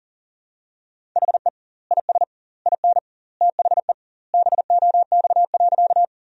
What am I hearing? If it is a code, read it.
Morse code, heard as a keyed tone, about 40 wpm, 700 Hz. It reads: HE IS IN THE BOX.
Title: 2x Word-Spacing - Sentences from Top 500 Words - 40wpm